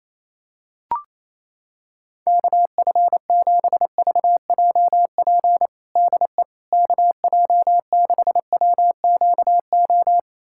Morse code, heard as a keyed tone, about 28 wpm, 700 Hz. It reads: KF7VJP DE KJ6WQO
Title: USA Callsign DE Callsign 28wpm